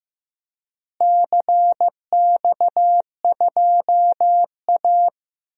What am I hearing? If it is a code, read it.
Morse code, heard as a keyed tone, about 15 wpm, 700 Hz. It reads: CX2A
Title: DXCC-100 Call Signs - 4 Characters - Repeated 3x - 15wpm